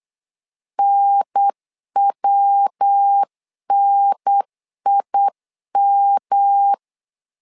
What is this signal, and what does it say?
Morse code, heard as a keyed tone, about 8 wpm, 800 Hz. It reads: NWNIM
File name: LEZIONI / 3 MWRO / 3 MWR 8wpm 2444